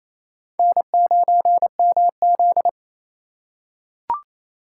Morse code, heard as a keyed tone, about 28 wpm, 700 Hz. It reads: N9MZ
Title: USA Call Signs - 4 Characters - 2x Character Spacing - 28wpm